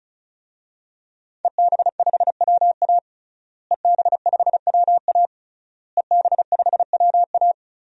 Morse code, heard as a keyed tone, about 34 wpm, 700 Hz. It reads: EB5WA EB5WA EB5WA
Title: DXCC-100 Call Signs - 5 Characters - Repeated 3x - 35wpm